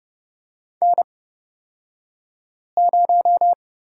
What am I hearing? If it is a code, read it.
Morse code, heard as a keyed tone, about 31 wpm, spaced as for 5 wpm, 700 Hz. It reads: N0
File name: Lesson 38d - Letter q - Two Characters 30wpm